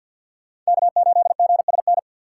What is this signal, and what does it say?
Morse code, heard as a keyed tone, about 46 wpm, 700 Hz. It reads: K8GSN
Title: USA Call Signs - 5 Characters - 2x Character Spacing - 50wpm